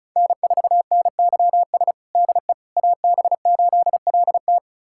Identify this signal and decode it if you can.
Morse code, heard as a keyed tone, about 36 wpm, 700 Hz. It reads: N4NYS DE AB8LT
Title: USA Callsign DE Callsign - 5 Characters - 35wpm